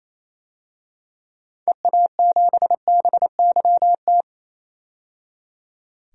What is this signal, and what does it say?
Morse code, heard as a keyed tone, about 28 wpm, 700 Hz. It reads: EA7BYT
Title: DXCC-100 Call Signs - 6 Characters - 28wpm